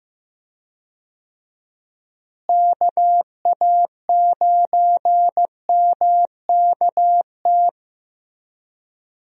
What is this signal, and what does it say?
Morse code, heard as a keyed tone, about 15 wpm, 700 Hz. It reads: KA9MKT
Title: USA Call Signs - 6 Characters - Repeated 2x - 15wpm